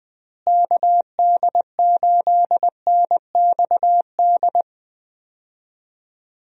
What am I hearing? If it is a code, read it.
Morse code, heard as a keyed tone, about 20 wpm, 700 Hz. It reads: KD8NXD
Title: USA Call Signs - 6 Characters - Repeated 3x - 20wpm